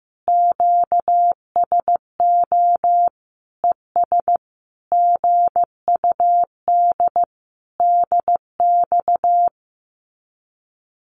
Morse code, heard as a keyed tone, about 15 wpm, 700 Hz. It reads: QSO ES GUD DX
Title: Common QSO Phrases Encore 15wpm